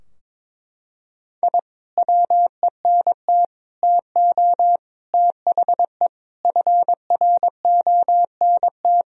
Morse code, heard as a keyed tone, about 22 wpm, 700 Hz. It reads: I WENT TO THE FRONT
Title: Sentences from Top 1000 Words - Review Part 2 - 22wpm